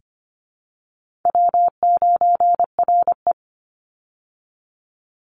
Morse code, heard as a keyed tone, about 25 wpm, 700 Hz. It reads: W9RE
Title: Easy Call Signs Repeated 3x - 25wpm